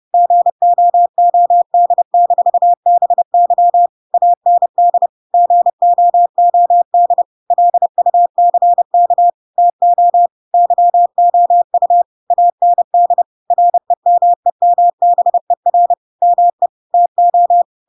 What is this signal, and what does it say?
Morse code, heard as a keyed tone, about 30 wpm, 700 Hz. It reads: GOOD-BY AND GOOD LUCK TO YOU AND REMEMBER ME TO
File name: Pinocchio / Pinocchio ch5 30 30 ik1hge